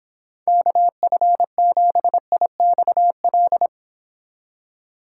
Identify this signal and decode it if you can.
Morse code, heard as a keyed tone, about 26 wpm, 700 Hz. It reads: KF7IXL